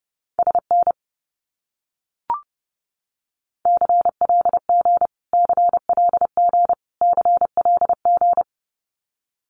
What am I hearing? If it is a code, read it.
Morse code, heard as a keyed tone, about 30 wpm, 700 Hz. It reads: SN CLG CLG CLG
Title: Abbreviations Repeated 3x - 30wpm